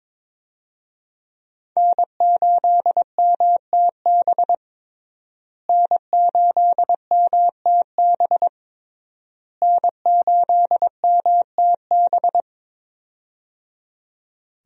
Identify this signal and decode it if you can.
Morse code, heard as a keyed tone, about 22 wpm, 700 Hz. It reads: N8MTB N8MTB N8MTB
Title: USA Call Signs - 5 Characters - Repeated 3x - 22wpm